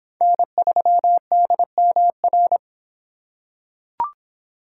Morse code, heard as a keyed tone, about 23 wpm, 700 Hz. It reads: N3DMR